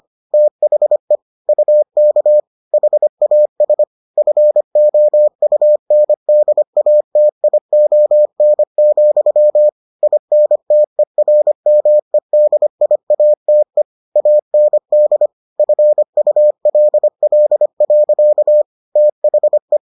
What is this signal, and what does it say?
Morse code, heard as a keyed tone, about 25 wpm, 600 Hz. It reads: THE UK HAS FOUNDATION, INTERMEDIATE AND FULL. THE